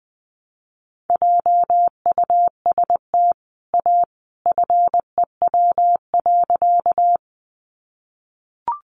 Morse code, heard as a keyed tone, about 20 wpm, 700 Hz. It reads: JUST A FEW.
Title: Sentences from Top 300 Words 20wpm